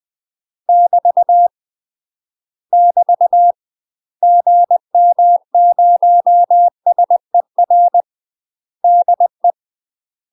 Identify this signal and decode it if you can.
Morse code, heard as a keyed tone, about 20 wpm, 700 Hz. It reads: <BT> <BT> GM0SER DE
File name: UK MASTER QSOs TRAINING/20WPM